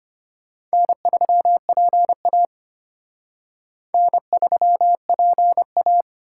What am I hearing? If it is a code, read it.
Morse code, heard as a keyed tone, about 29 wpm, 700 Hz. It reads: N3PA N3PA
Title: Speed Racing - USA 1x2 Call Signs 25wpm